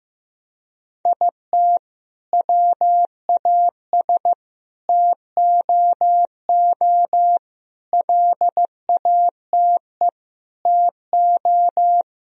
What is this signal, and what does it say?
Morse code, heard as a keyed tone, about 15 wpm, 700 Hz. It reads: IT WAS TOO LATE TO
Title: Sentences from Top 1000 Words - Review Part 1 - 15wpm